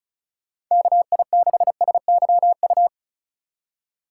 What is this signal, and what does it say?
Morse code, heard as a keyed tone, about 44 wpm, 700 Hz. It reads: KI6SYU
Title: Speed Racing - USA 2x3 Call Signs 35wpm